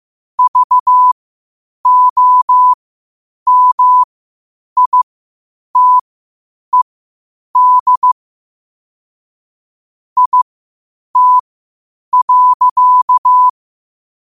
Morse code, heard as a keyed tone, about 15 wpm, spaced as for 8 wpm, 1000 Hz. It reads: VOMITED IT.
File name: qotdfiles / 8wpm